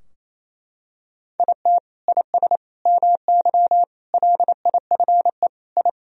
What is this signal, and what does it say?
Morse code, heard as a keyed tone, about 28 wpm, 700 Hz. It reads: IT IS MY LIFE I
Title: Sentences from Top 1000 Words - Review Part 2 - 28wpm